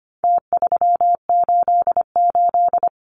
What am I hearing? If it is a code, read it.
Morse code, heard as a keyed tone, about 25 wpm, 700 Hz. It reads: T388
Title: Numbers - Contest Style - 4-Digit 25wpm